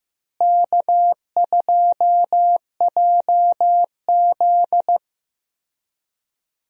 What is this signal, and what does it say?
Morse code, heard as a keyed tone, about 16 wpm, 700 Hz. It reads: K2JZ
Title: USA Call Signs - 4 Characters - Repeated 3x - 15wpm